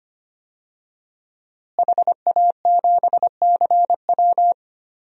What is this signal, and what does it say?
Morse code, heard as a keyed tone, about 25 wpm, 700 Hz. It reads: HA7CW
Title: DXCC-100 Call Signs - 5 Characters - Repeated 3x - 25wpm